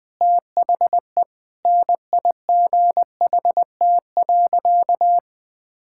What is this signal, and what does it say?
Morse code, heard as a keyed tone, about 20 wpm, 700 Hz. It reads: THE NIGHT.
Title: Sentences from Top 1000 Words - Review Part 6 - 20wpm